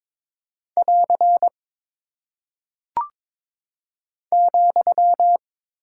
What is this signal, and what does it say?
Morse code, heard as a keyed tone, about 22 wpm, 700 Hz. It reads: <AR> ,
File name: practice / 22wpm / Single Punctuation-Prosign - 22wpm